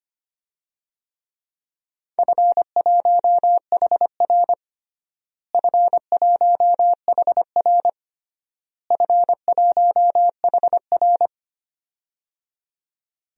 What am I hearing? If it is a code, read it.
Morse code, heard as a keyed tone, about 25 wpm, 700 Hz. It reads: F1HR F1HR F1HR
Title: DXCC-100 Call Signs - 4 Characters - Repeated 3x - 25wpm